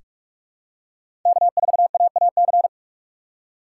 Morse code, heard as a keyed tone, about 46 wpm, 700 Hz. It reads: K4AAC
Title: K1USN SST QSO Practice - 45wpm